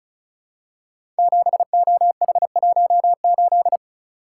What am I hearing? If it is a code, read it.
Morse code, heard as a keyed tone, about 36 wpm, 700 Hz. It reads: 7OH18
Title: Code Groups - 5 Letter:Numbers - 35wpm